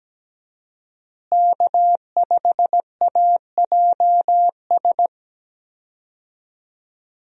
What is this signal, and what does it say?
Morse code, heard as a keyed tone, about 17 wpm, 700 Hz. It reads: K5AJS
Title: USA Call Signs - 5 Characters - Repeated 3x - 17wpm